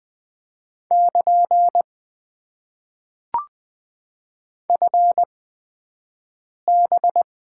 Morse code, heard as a keyed tone, about 20 wpm, spaced as for 5 wpm, 700 Hz. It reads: <KN> FB
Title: Top 90 QSO Elements - 8x Character Spacing - 20wpm